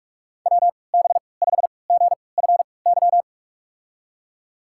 Morse code, heard as a keyed tone, about 43 wpm, spaced as for 25 wpm, 700 Hz. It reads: WB5GFY
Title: USA Call Signs - 3x Character Spacing - 45wpm